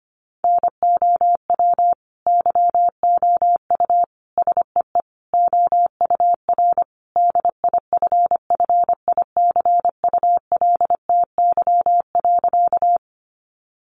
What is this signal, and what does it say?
Morse code, heard as a keyed tone, about 25 wpm, 700 Hz. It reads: NOW YOU SEE OUR DIFFICULTY.
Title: Sentences from Top 1500 Words 25wpm